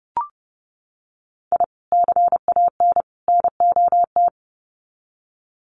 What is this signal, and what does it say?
Morse code, heard as a keyed tone, about 30 wpm, 700 Hz. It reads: I CAN NOT
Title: Sets of 3 Words 30wpm